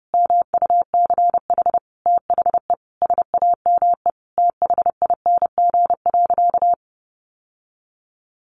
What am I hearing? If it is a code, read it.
Morse code, heard as a keyed tone, about 30 wpm, 700 Hz. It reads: MUCH THE SAME THING.
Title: Sentences from Top 1100 Words 30wpm